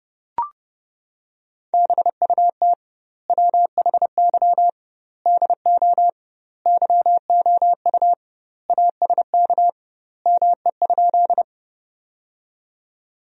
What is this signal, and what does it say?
Morse code, heard as a keyed tone, about 30 wpm, 700 Hz. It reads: BUT WHY DO YOU ASK ME?
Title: 2x Word-Spacing - Sentences from Top 200 Words - 30wpm